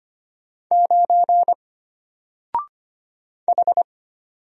Morse code, heard as a keyed tone, about 25 wpm, 700 Hz. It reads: 9 H
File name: Single-Letter-Number-25wpm